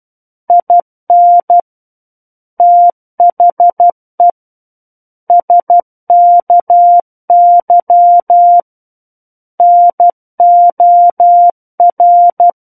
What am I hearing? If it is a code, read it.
Morse code, heard as a keyed tone, about 12 wpm, 700 Hz. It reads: IN THE SKY NOR
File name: Book One / 03-Chapter 3 - 12 WPM